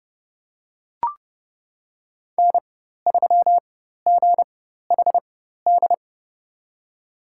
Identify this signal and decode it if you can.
Morse code, heard as a keyed tone, about 30 wpm, spaced as for 14 wpm, 700 Hz. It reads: N3GHD
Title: USA Call Signs - 5 Characters - 4x Character Spacing - 30wpm